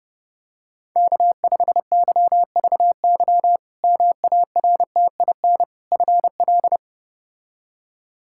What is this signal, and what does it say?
Morse code, heard as a keyed tone, about 30 wpm, 700 Hz. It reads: K5YVY MARTIN FL
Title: K1USN SST QSO Practice - 30wpm